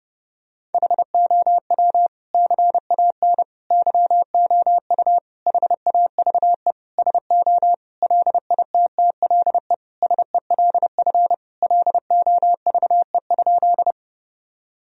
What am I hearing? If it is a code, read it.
Morse code, heard as a keyed tone, about 30 wpm, 700 Hz. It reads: HOW CAN YOU HAVE SO LITTLE SELF LOVE?
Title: Sentences from Top 1000 Words - Review Part 1 - 30wpm